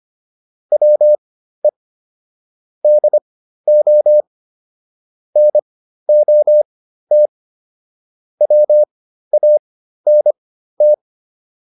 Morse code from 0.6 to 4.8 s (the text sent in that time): WE DO